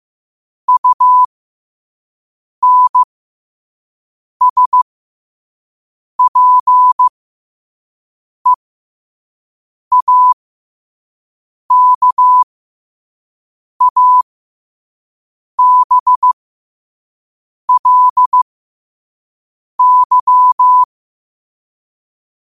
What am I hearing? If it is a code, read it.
Morse code, heard as a keyed tone, about 15 wpm, spaced as for 5 wpm, 1000 Hz. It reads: UNSPEAKABLY
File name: qotdfiles / 5wpm